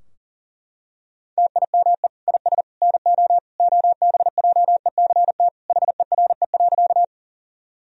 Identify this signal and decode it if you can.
Morse code, heard as a keyed tone, about 40 wpm, 700 Hz. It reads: TIME IS NO OBJECT HERE.
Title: Sentences from Top 1000 Words - Review Part 3 - 40wpm